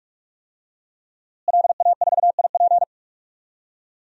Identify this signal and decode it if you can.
Morse code, heard as a keyed tone, about 46 wpm, 700 Hz. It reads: LA4IP
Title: DXCC-100 Call Signs - 5 Characters - Repeated 2x - 45wpm